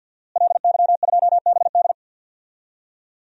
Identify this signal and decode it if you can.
Morse code, heard as a keyed tone, about 52 wpm, 700 Hz. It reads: LY2BD
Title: DXCC-100 Call Signs - 5 Characters - Repeated 3x - 50wpm